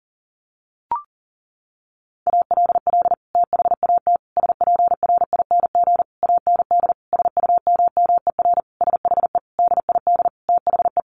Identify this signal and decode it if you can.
Morse code, heard as a keyed tone, about 40 wpm, 700 Hz. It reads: ALL THAT SPRING AND SUMMER SHE DID THE
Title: Sentences from Top 1200 Words 40wpm